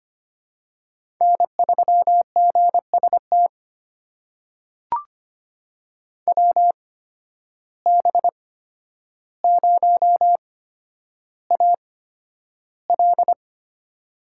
Morse code, heard as a keyed tone, about 25 wpm, spaced as for 11 wpm, 700 Hz. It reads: N3GST WB0AL
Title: USA Call Signs - 5 Characters - 8x Character Spacing - 25wpm